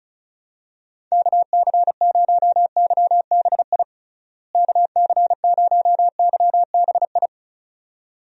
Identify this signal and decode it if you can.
Morse code, heard as a keyed tone, about 36 wpm, 700 Hz. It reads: KC0YBI KC0YBI
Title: USA Call Signs - 6 Characters - Repeated 2x - 35wpm